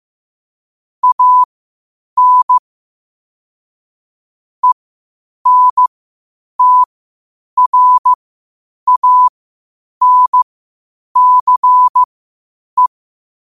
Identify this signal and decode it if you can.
Morse code, heard as a keyed tone, about 15 wpm, spaced as for 8 wpm, 1000 Hz. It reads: AN ENTRANCE
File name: 8wpm